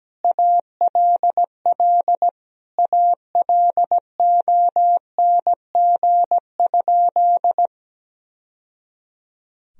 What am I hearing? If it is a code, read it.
Morse code, heard as a keyed tone, about 17 wpm, 700 Hz. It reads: ALL ALONG?
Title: Sentences from Top 1000 Words - Review Part 3 - 17wpm